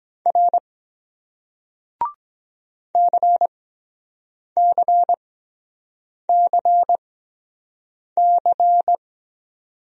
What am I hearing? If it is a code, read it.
Morse code, heard as a keyed tone, about 23 wpm, spaced as for 7 wpm, 700 Hz. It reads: R CCCC